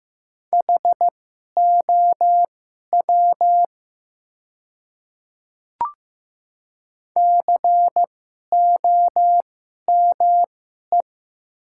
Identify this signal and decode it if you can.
Morse code, heard as a keyed tone, about 15 wpm, spaced as for 11 wpm, 700 Hz. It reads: HOW COME
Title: 2x Character Spacing - Top 100 Words - 15wpm